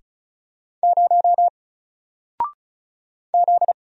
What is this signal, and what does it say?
Morse code, heard as a keyed tone, about 35 wpm, 700 Hz. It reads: 0 Z
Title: Single Letter-Number 35wpm